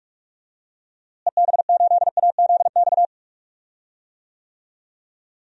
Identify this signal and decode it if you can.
Morse code, heard as a keyed tone, about 44 wpm, 700 Hz. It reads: EB8AZX
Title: DXCC-100 Call Signs - 6 Characters - Repeated 2x - 45wpm